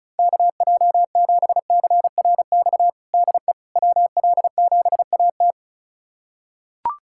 Morse code, heard as a keyed tone, about 34 wpm, 700 Hz. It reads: KJ7CRX DE WL7AT
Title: USA Callsign DE Callsign 35wpm